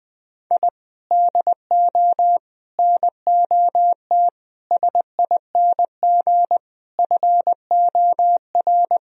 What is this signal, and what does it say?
Morse code, heard as a keyed tone, about 20 wpm, 700 Hz. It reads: I DO NOT SING FOR
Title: Sentences from Top 1000 Words - Review Part 2 - 20wpm